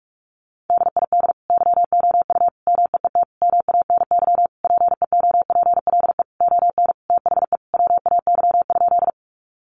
Morse code, heard as a keyed tone, about 44 wpm, 700 Hz. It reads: DID YOU MEET MANY PEOPLE ON THE WAY?
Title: Sentences from Top 1100 Words 45wpm